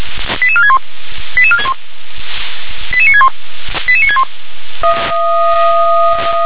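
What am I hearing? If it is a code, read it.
Five-tone selective call (ZVEI), heard as quick runs of five tones, tones of about 70 ms. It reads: 88461 88961 88961
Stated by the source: ZVEI five-tone call received by SDR.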